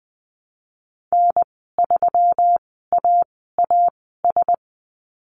Morse code, heard as a keyed tone, about 19 wpm, spaced as for 14 wpm, 700 Hz. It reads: N3AAS
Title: 140 Lesson 23g - Letter f - Call Signs 20wpm